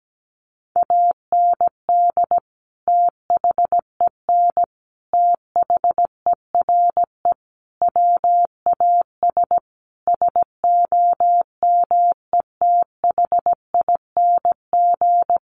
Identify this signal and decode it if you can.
Morse code, heard as a keyed tone, about 17 wpm, 700 Hz. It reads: AND THEN THERE WAS SOMETHING
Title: Sentences from Top 1400 Words 17wpm